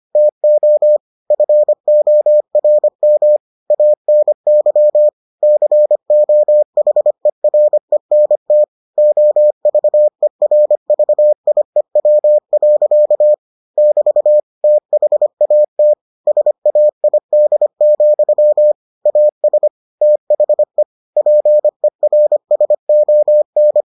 Morse code, heard as a keyed tone, about 25 wpm, 600 Hz. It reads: TO FORM ANY COHERENT OVERVIEW. <BT> THAT SAID, AS THE PERSON